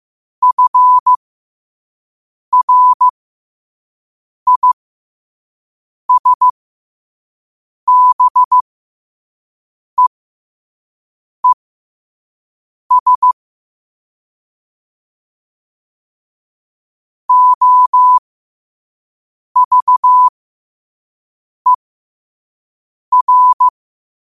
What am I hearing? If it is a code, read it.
Morse code, heard as a keyed tone, about 15 wpm, spaced as for 5 wpm, 1000 Hz. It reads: FRISBEES OVER